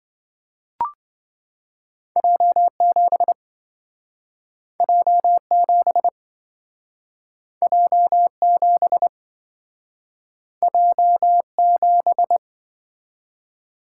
Morse code, heard as a keyed tone, about 27 wpm, 700 Hz. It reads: J7 J7 J7 J7